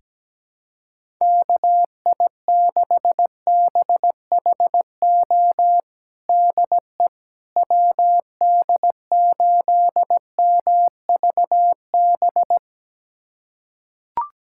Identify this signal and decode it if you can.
Morse code, heard as a keyed tone, about 17 wpm, 700 Hz. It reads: KI6BHO DE WD8MVB
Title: USA Callsign DE Callsign 17wpm